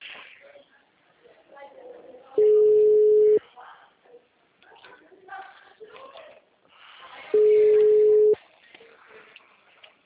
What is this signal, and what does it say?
Ringback tone (Europe, 425 Hz), with speech.